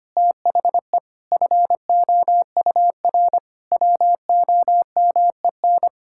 Morse code, heard as a keyed tone, about 25 wpm, 700 Hz. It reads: THE FOUR WOMEN
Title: Pangram Challenge v2 - 25wpm